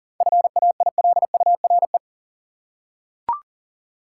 Morse code, heard as a keyed tone, about 40 wpm, 700 Hz. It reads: FAILURE
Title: Seven Letter Words - 40wpm